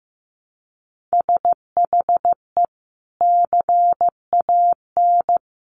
Morse code, heard as a keyed tone, about 15 wpm, 700 Hz. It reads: SHE CAN
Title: Sentences from Top 700 Words 15wpm